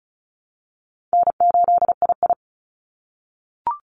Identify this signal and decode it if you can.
Morse code, heard as a keyed tone, about 34 wpm, 700 Hz. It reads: N8II
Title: Call Signs - 35wpm